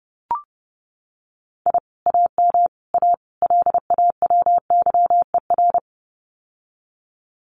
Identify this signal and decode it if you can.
Morse code, heard as a keyed tone, about 28 wpm, 700 Hz. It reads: I AM A LAWYER
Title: Common QSO Phrases Encore 30wpm